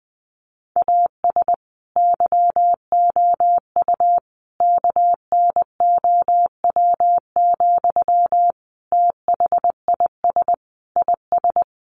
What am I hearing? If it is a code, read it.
Morse code, heard as a keyed tone, about 19 wpm, 700 Hz. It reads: AS YOU KNOW, THIS IS
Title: Sentences from Top 1100 Words 20wpm